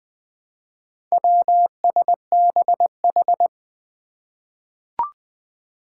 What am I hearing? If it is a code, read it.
Morse code, heard as a keyed tone, about 20 wpm, 700 Hz. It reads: WSBH
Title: Four Challenging Letter-Number Combinations 20wpm